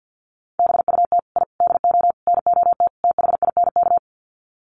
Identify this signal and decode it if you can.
Morse code, heard as a keyed tone, about 50 wpm, 700 Hz. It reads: BUT I DO NOT THINK
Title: Sets of 5 Words 50wpm